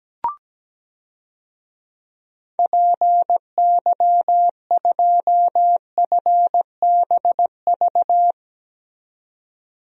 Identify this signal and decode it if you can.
Morse code, heard as a keyed tone, about 17 wpm, 700 Hz. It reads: PY2FBV
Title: DXCC-100 Call Signs - 6 Characters - 17wpm